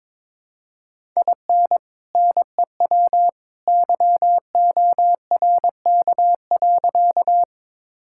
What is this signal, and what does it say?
Morse code, heard as a keyed tone, about 22 wpm, 700 Hz. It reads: IN NEW YORK.
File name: practice / 22wpm / Sentences with 2-Word Phrases - 22wpm